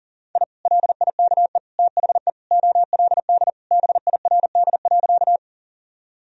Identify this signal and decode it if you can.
Morse code, heard as a keyed tone, about 40 wpm, 700 Hz. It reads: I LIKE THE OLD BIRD.
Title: Sentences from Top 1000 Words - Review Part 2 - 40wpm